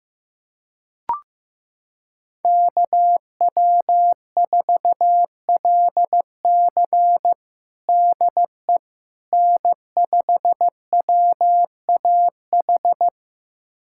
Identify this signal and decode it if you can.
Morse code, heard as a keyed tone, about 15 wpm, 700 Hz. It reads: KW4LC DE N5WAH
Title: USA Callsign DE Callsign - 5 Characters - 15wpm